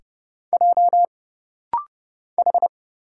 Morse code, heard as a keyed tone, about 29 wpm, 700 Hz. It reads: J H
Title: Single Letter-Number Alt - Rapid-Fire 30wpm